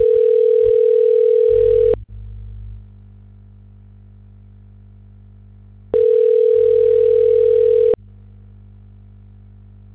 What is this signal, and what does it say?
Ringback tone (North America), 440 + 480 Hz.